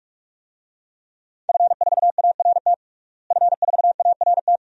Morse code, heard as a keyed tone, about 46 wpm, 700 Hz. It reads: F4ART F4ART
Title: DXCC-100 Call Signs - 5 Characters - Repeated 2x - 45wpm